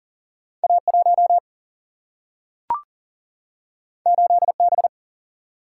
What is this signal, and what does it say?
Morse code, heard as a keyed tone, about 40 wpm, 700 Hz. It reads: A1 8B
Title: Single-Letter Single-Number 40wpm